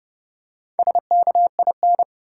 Morse code, heard as a keyed tone, about 30 wpm, 700 Hz. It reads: SKIN
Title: Four Letter Words - 30wpm